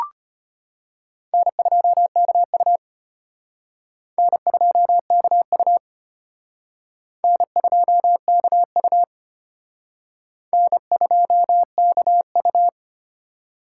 Morse code, heard as a keyed tone, about 32 wpm, 700 Hz. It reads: N2KU N2KU N2KU N2KU